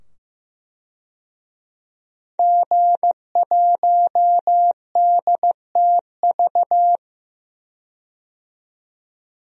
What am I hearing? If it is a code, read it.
Morse code, heard as a keyed tone, about 15 wpm, 700 Hz. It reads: G1DTV
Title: DXCC-100 Call Signs - 5 Characters - 15wpm